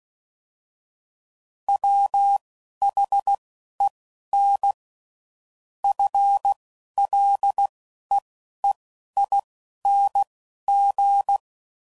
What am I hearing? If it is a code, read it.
Morse code, heard as a keyed tone, about 16 wpm, spaced as for 11 wpm, 800 Hz. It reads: WHEN FLEEING